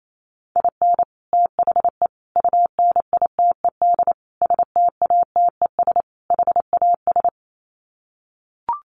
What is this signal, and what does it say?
Morse code, heard as a keyed tone, about 28 wpm, 700 Hz. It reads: IN THE UNITED STATES HAS
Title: Sets of 5 Words Encore2 28wpm